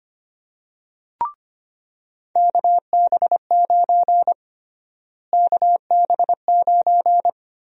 Morse code, heard as a keyed tone, about 25 wpm, 700 Hz. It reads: KB9 KB9
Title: USA Call Sign Prefixes - Repeated 2x - 25wpm